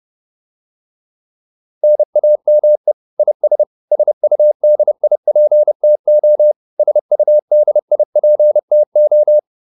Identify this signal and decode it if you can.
Morse code, heard as a keyed tone, about 30 wpm, 600 Hz. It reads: NAME IS SUDIPTO SUDIPTO
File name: QSO410 30